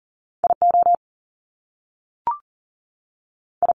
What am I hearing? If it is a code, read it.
Morse code, heard as a keyed tone, about 39 wpm, 700 Hz. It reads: I O I
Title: Top Two Letter Combinations in Words 40wpm